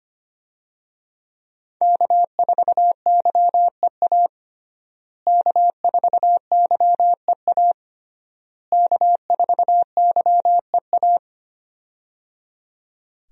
Morse code, heard as a keyed tone, about 25 wpm, 700 Hz. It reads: K4YEA K4YEA K4YEA
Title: USA Call Signs - 5 Characters - Repeated 3x - 25wpm